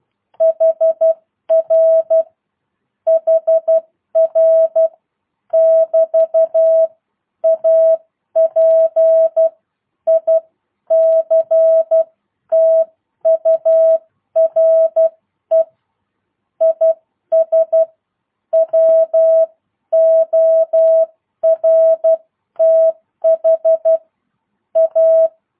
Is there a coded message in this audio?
Yes: Morse code from 0.4 to 25.3 s, heard as a keyed tone, about 12 wpm, spaced as for 10 wpm, 650 Hz. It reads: HR HR <BT> A PICTURE IS WORTH A